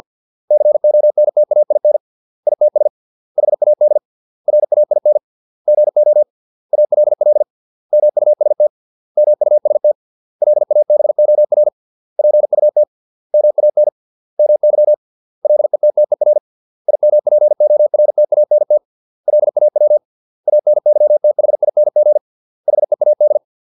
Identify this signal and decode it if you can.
Morse code, heard as a keyed tone, about 50 wpm, 600 Hz. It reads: CONTAIN ITS HAD RAIN GO ALL MUST MUST LABOR PUT MAN MY LETTER IMPORTANT LAW ANYTHING HEAD